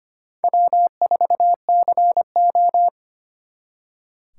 Morse code, heard as a keyed tone, about 31 wpm, 700 Hz. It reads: W4CO